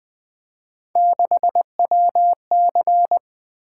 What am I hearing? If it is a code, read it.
Morse code, heard as a keyed tone, about 20 wpm, 700 Hz. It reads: 6WC
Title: Three Letter-Number Combinations - 20wpm